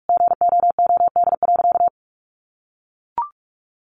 Morse code, heard as a keyed tone, about 44 wpm, 700 Hz. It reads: GOOD.
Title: Sentences from Top 1300 Words 45wpm